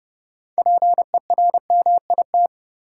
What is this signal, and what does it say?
Morse code, heard as a keyed tone, about 30 wpm, 700 Hz. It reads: PERMIT
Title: Six Letter Words - 30wpm